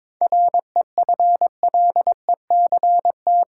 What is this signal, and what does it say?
Morse code, heard as a keyed tone, about 22 wpm, 700 Hz. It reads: REFLECT
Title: Seven Letter Words - 22wpm